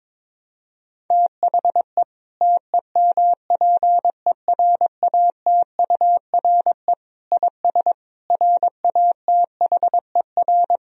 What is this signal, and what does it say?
Morse code, heard as a keyed tone, about 22 wpm, 700 Hz. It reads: THE TEMPERATURE IS RATHER
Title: Sentences from Top 1000 Words - Review Part 4 - 22wpm